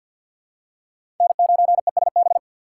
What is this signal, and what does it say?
Morse code, heard as a keyed tone, about 53 wpm, 700 Hz. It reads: N9ESB
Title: USA Call Signs - 2x Character Spacing - 50wpm